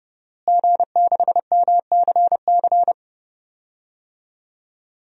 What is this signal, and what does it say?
Morse code, heard as a keyed tone, about 30 wpm, 700 Hz. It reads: G6MCC
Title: DXCC-100 Call Signs - 5 Characters - Repeated 2x - 30wpm